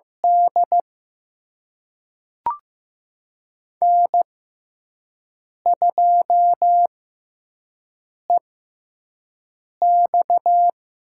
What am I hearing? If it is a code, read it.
Morse code, heard as a keyed tone, about 15 wpm, spaced as for 5 wpm, 700 Hz. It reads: D N2EX
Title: USA Call Signs - 4 Characters - 6x Character Spacing - 15wpm